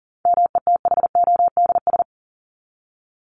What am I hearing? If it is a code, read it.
Morse code, heard as a keyed tone, about 40 wpm, 700 Hz. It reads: METHODS
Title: ES:S Suffix - Top 100 Words with Base Words 40wpm